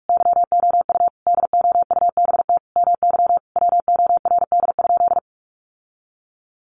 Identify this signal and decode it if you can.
Morse code, heard as a keyed tone, about 44 wpm, 700 Hz. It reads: YOU DOUBT MY WORD?
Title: Sentences from Top 1500 Words 45wpm